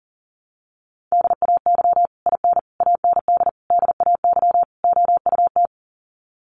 Morse code, heard as a keyed tone, about 37 wpm, 700 Hz. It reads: DAY IN AND DAY OUT
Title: Sets of 5 Words Encore2 40wpm